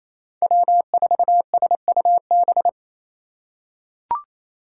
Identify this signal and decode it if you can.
Morse code, heard as a keyed tone, about 28 wpm, 700 Hz. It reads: W4SUB
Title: USA Call Signs - 3x Character Spacing - 28wpm